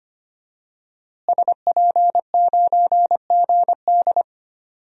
Morse code, heard as a keyed tone, about 25 wpm, 700 Hz. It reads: SP9GD
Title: DXCC-100 Call Signs - 5 Characters - Repeated 3x - 25wpm